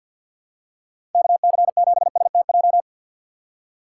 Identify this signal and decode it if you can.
Morse code, heard as a keyed tone, about 47 wpm, 700 Hz. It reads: KC7STJ
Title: USA Call Signs - 6 Characters - 4x Character Spacing - 50wpm